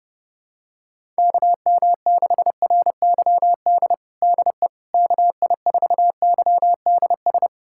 Morse code, heard as a keyed tone, about 32 wpm, 700 Hz. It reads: KM6RYD DE KI4YDS
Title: USA Callsign DE Callsign - 6 Characters - 30wpm